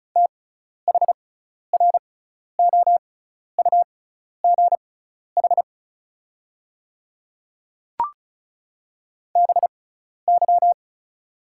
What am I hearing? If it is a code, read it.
Morse code, heard as a keyed tone, about 36 wpm, spaced as for 12 wpm, 700 Hz. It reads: THROUGH BY
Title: 6x Character Spacing - Top 200 Words - 35wpm